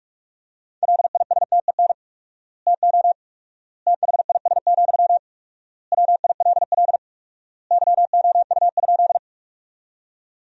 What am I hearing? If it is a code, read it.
Morse code, heard as a keyed tone, about 46 wpm, 700 Hz. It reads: LISTEN TO THIS, WILL YOU?
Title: 4x Word-Spacing - Sentences from Top 400 Words - 45wpm